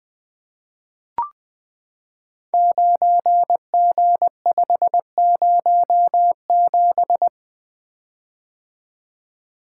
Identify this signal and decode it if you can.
Morse code, heard as a keyed tone, about 20 wpm, 700 Hz. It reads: 9G507
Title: Code Groups - 5 Letter:Numbers - 20wpm